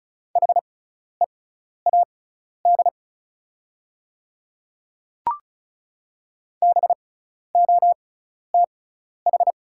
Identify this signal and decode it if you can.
Morse code, heard as a keyed tone, about 36 wpm, spaced as for 12 wpm, 700 Hz. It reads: HEAD BOTH
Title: 6x Character Spacing - Top 300 Words - 35wpm